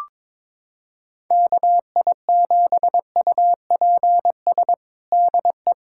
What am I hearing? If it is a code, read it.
Morse code, heard as a keyed tone, about 21 wpm, 700 Hz. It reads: KI7UPS DE
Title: USA Callsign DE Callsign - 6 Characters - 22wpm